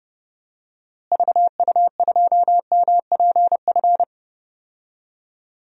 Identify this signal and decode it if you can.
Morse code, heard as a keyed tone, about 32 wpm, 700 Hz. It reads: VU2MPF
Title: DXCC-100 Call Signs - 6 Characters - Repeated 2x - 30wpm